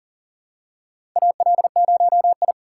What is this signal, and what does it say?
Morse code, heard as a keyed tone, about 40 wpm, 700 Hz. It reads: AL0I
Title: USA Call Signs - 4 Characters - 4x Character Spacing - 40wpm